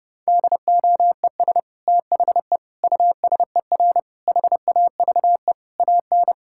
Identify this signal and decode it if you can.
Morse code, heard as a keyed tone, about 30 wpm, 700 Hz. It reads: DOES THE USER HAVE AN
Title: Sentences from Top 1000 Words - Review Part 5 - 30wpm